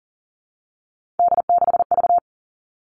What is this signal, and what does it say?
Morse code, heard as a keyed tone, about 39 wpm, 700 Hz. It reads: D6V
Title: Three Challenging Letter-Number Combinations 40wpm